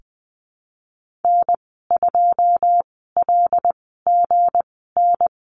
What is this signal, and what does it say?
Morse code, heard as a keyed tone, about 20 wpm, spaced as for 15 wpm, 700 Hz. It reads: N2LGN
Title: 207 Lesson 33g - Letter j - Call Signs 20wpm